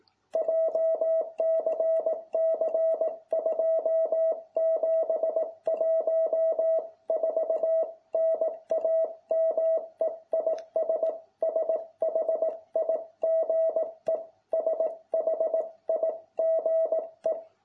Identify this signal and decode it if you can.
Morse code, heard as a keyed tone, about 18 wpm, 660 Hz. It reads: JCC 2714 NAME IS SHIGE SHIGE